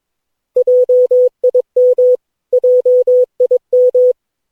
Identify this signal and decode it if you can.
Morse code, heard as a keyed tone, about 22 wpm, 500 Hz. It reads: JIM JIM
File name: cw / Session9 cw training